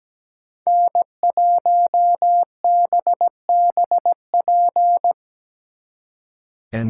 Morse code, heard as a keyed tone, about 19 wpm, 700 Hz. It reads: N1BBP
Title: Speed Racing - USA 1x3 Call Signs 17wpm